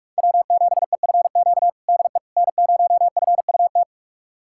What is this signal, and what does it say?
Morse code, heard as a keyed tone, about 46 wpm, 700 Hz. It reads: W7EFQ DE N0FUT
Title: USA Callsign DE Callsign - 5 Characters - 45wpm